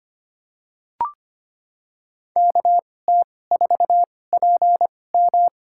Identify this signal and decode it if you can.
Morse code, heard as a keyed tone, about 25 wpm, spaced as for 18 wpm, 700 Hz. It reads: KT4PM
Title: USA Call Signs - 2x Character Spacing - 25wpm